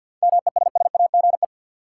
Morse code, heard as a keyed tone, about 50 wpm, 700 Hz. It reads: MESSAGE
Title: Sentences from Top 1100 Words - New Word Drill - 50wpm